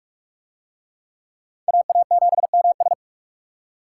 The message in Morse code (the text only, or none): AA7MS